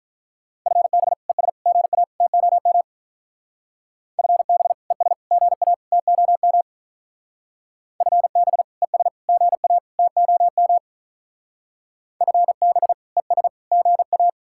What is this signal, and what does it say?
Morse code, heard as a keyed tone, about 42 wpm, 700 Hz. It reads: FB ES GA TOM FB ES GA TOM FB ES GA TOM FB ES GA
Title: Speed-Racing Common QSO Phrases - 35wpm